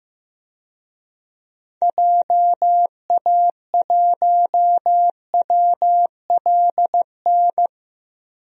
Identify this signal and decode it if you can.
Morse code, heard as a keyed tone, about 15 wpm, 700 Hz. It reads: JA1WLN
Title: DXCC-100 Call Signs - 6 Characters - Repeated 2x - 15wpm